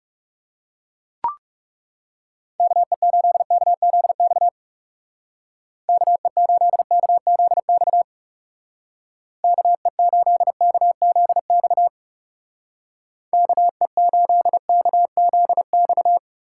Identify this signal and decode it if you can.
Morse code, heard as a keyed tone, about 36 wpm, 700 Hz. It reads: KE8KZX KE8KZX KE8KZX KE8KZX
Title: Speed Racing - USA 2x3 Call Signs 30wpm